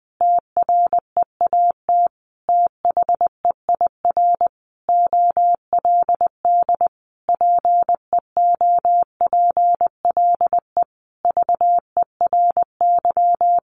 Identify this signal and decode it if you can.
Morse code, heard as a keyed tone, about 20 wpm, 700 Hz. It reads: TREAT THEIR OLD PEOPLE VERY
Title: Sentences from Top 1300 Words 20wpm